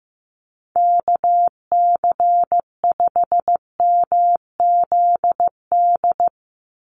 Morse code, heard as a keyed tone, about 16 wpm, 700 Hz. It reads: KC5MZD
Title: USA Call Signs 15wpm